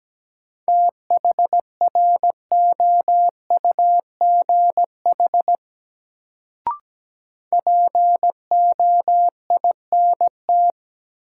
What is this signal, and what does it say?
Morse code, heard as a keyed tone, about 17 wpm, 700 Hz. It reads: THROUGH POINT
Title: Sentences from Top 200 Words - New Word Drill - 17wpm